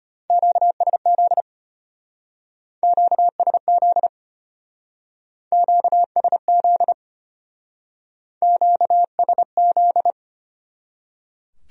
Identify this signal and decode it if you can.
Morse code, heard as a keyed tone, about 30 wpm, 700 Hz. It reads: QSZ QSZ QSZ QSZ
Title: Speed-Racing Q-Codes - 25wpm